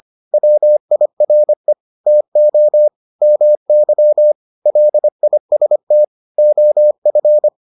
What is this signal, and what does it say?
Morse code, heard as a keyed tone, about 25 wpm, 600 Hz. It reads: WIRE TO MY LIST OF